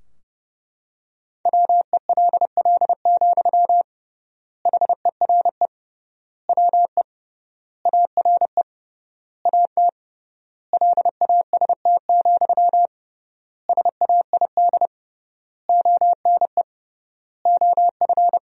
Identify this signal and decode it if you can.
Morse code, heard as a keyed tone, about 32 wpm, 700 Hz. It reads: WELL, HERE WE ARE AT LAST, SAID ONE OF
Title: 3x Word-Spacing - Sentences from Top 300 Words - 30wpm